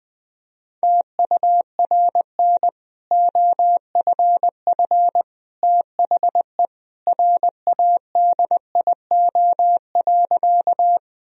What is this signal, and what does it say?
Morse code, heard as a keyed tone, about 20 wpm, 700 Hz. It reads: TURN OFF THE RADIO.
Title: Sentences from Top 1000 Words - Review Part 3 - 20wpm